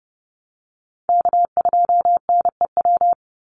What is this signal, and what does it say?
Morse code, heard as a keyed tone, about 30 wpm, 700 Hz. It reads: K2NEW
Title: USA Call Signs 30wpm